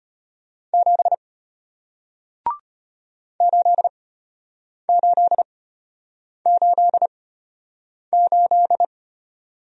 Morse code, heard as a keyed tone, about 34 wpm, spaced as for 8 wpm, 690 Hz. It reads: 7 8888